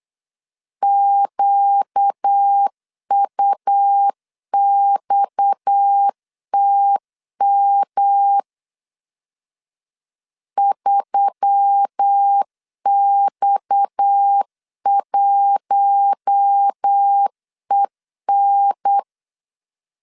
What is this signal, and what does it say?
Morse code, heard as a keyed tone, about 8 wpm, 800 Hz. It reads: QUXTM 3X1EN
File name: LEZIONI / 6 PKX0 / 6 PKX 8wpm 3041